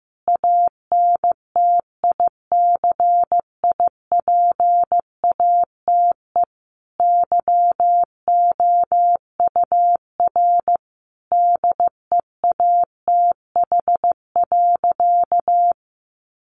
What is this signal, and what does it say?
Morse code, heard as a keyed tone, about 15 wpm, 700 Hz. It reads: ANTICIPATE YOUR DEATH.
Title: Sentences from Top 1500 Words 15wpm